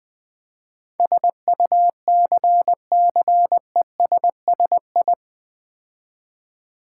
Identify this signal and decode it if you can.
Morse code, heard as a keyed tone, about 20 wpm, 700 Hz. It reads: SUCCESSI
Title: Build A Word - 4 Letters At A Time - 10-Letter Words - 20wpm